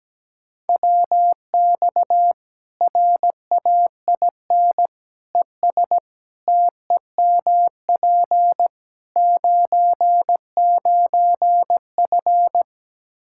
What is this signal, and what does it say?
Morse code, heard as a keyed tone, about 17 wpm, 700 Hz. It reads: WX RAIN ES TEMP 99F
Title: Practice QSO - Exchange 2 - Info - 17wpm